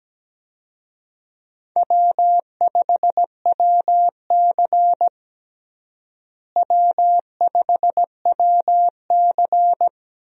Morse code, heard as a keyed tone, about 17 wpm, 700 Hz. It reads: W5WC W5WC
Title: USA Call Signs - 4 Characters - Repeated 3x - 17wpm